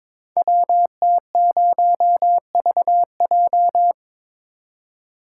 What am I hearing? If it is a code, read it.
Morse code, heard as a keyed tone, about 23 wpm, 700 Hz. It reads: WT0VJ
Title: Code Groups - 5 Letter:Numbers - 22wpm